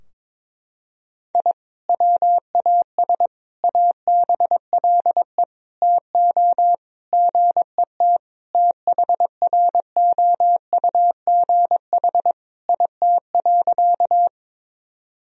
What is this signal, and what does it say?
Morse code, heard as a keyed tone, about 22 wpm, 700 Hz. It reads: I WAS ABLE TO GET THROUGH IT.
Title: Sentences from Top 1000 Words - Review Part 2 - 22wpm